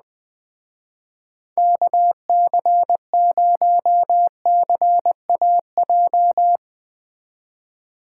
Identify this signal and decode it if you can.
Morse code, heard as a keyed tone, about 20 wpm, 700 Hz. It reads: KC0CAJ
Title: Speed Racing - USA 2x3 Call Signs 17wpm